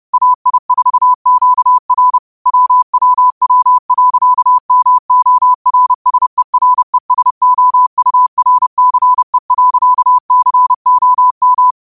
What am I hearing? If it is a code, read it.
Morse code, heard as a keyed tone, about 30 wpm, 1000 Hz. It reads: AI4QR WWW.MORSERESOURCE.COM
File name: qotdfiles / 30wpm